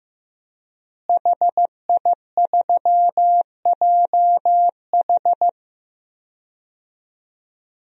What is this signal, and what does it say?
Morse code, heard as a keyed tone, about 16 wpm, 700 Hz. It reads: HI3JH
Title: DXCC-100 Call Signs - 5 Characters - Repeated 2x - 15wpm